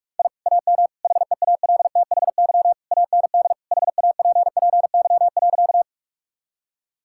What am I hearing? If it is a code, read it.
Morse code, heard as a keyed tone, about 41 wpm, 700 Hz. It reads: I AM HEALTHY AND HAPPY.
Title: Sentences from Top 1000 Words - Review Part 5 - 45wpm